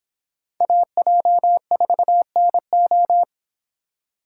Morse code, heard as a keyed tone, about 31 wpm, 700 Hz. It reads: AJ4NO